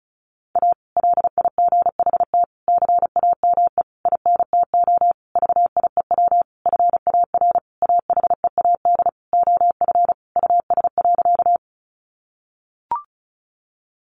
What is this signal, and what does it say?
Morse code, heard as a keyed tone, about 34 wpm, 700 Hz. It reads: A LIGHT CAME INTO VIEW FAR AHEAD OF US.
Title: Sentences from Top 1100 Words 35wpm